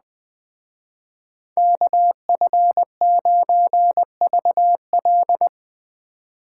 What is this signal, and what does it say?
Morse code, heard as a keyed tone, about 20 wpm, 700 Hz. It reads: KF9VL